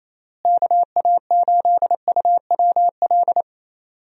Morse code, heard as a keyed tone, about 28 wpm, 700 Hz. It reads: KA8UWL
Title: USA Call Signs - 6 Characters - Repeated 3x - 28wpm